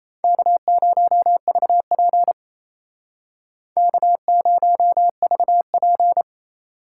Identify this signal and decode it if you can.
Morse code, heard as a keyed tone, about 36 wpm, 700 Hz. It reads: K0VP K0VP